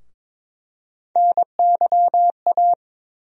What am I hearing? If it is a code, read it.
Morse code, heard as a keyed tone, about 22 wpm, 700 Hz. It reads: NYA
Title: USA Call Sign Postfixes - 3 Letters 22wpm